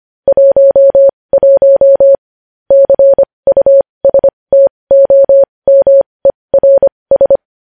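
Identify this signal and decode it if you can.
Morse code, heard as a keyed tone, about 25 wpm, spaced as for 20 wpm, 550 Hz. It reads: 11 CUSTOMERS
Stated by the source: Morse code news headlines practice at 20 wpm.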